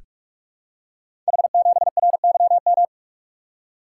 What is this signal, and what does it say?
Morse code, heard as a keyed tone, about 44 wpm, 690 Hz. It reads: H7RYM
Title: Code Groups - 5 Letter:Numbers - 45wpm